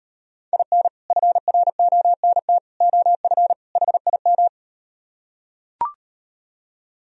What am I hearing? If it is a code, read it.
Morse code, heard as a keyed tone, about 38 wpm, 700 Hz. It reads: IN FRONT OF HIM